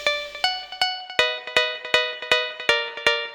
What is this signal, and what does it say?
There is no code.